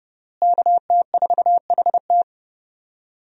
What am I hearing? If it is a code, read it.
Morse code, heard as a keyed tone, about 32 wpm, 700 Hz. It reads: KT4HT
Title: USA Call Signs - 5 Characters - Repeated 2x - 30wpm